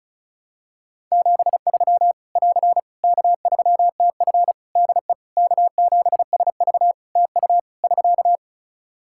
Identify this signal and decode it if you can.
Morse code, heard as a keyed tone, about 34 wpm, 700 Hz. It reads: 73 <AR> K3TF DE K7SV TU <SK>
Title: Practice QSO - Exchange 4 - Ending - 35wpm